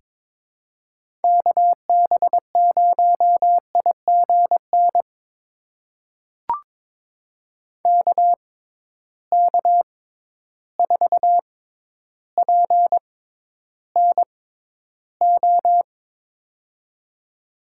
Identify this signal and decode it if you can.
Morse code, heard as a keyed tone, about 22 wpm, spaced as for 11 wpm, 700 Hz. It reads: KB0IGN KK4PNO
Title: USA Call Signs - 6 Characters - 6x Character Spacing - 22wpm